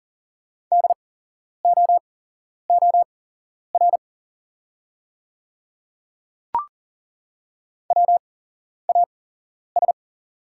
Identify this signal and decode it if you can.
Morse code, heard as a keyed tone, about 40 wpm, spaced as for 11 wpm, 700 Hz. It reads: DOOR WAS
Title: 8x Character Spacing - Top 500 Words - 40wpm